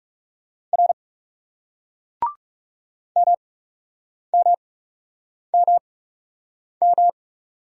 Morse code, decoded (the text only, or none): R MMMM